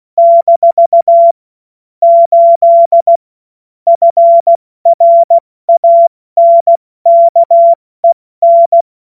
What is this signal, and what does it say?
Morse code, heard as a keyed tone, about 16 wpm, 680 Hz. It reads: - 8 FRANKEN
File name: Morsekurs / Witze / Witz 07. Juni 2022 70 F80